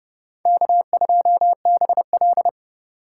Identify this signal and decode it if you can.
Morse code, heard as a keyed tone, about 30 wpm, 700 Hz. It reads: K2BL